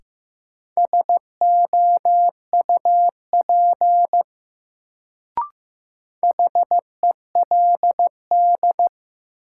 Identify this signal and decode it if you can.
Morse code, heard as a keyed tone, about 15 wpm, 700 Hz. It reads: SOUP HELD
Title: Sentences from Top 1600 Words - New Word Drill Alt - 15wpm